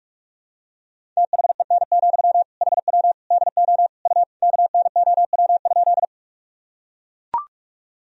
Morse code, heard as a keyed tone, about 44 wpm, 700 Hz. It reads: THEN, HW DO U KNOW?
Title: Sentences from Top 100 Words and Abbreviations 45wpm